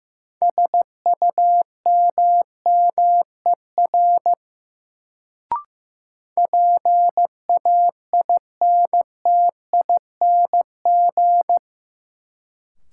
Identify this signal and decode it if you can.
Morse code, heard as a keyed tone, about 15 wpm, 700 Hz. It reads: SUMMER PAINTING
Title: Sentences from Top 1000 Words - Review Part 6 - New Word Drill - 15wpm